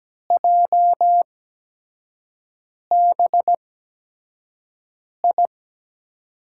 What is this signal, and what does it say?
Morse code, heard as a keyed tone, about 17 wpm, spaced as for 5 wpm, 700 Hz. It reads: JBI
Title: USA Call Signs - 6 Characters - 8x Character Spacing - 17wpm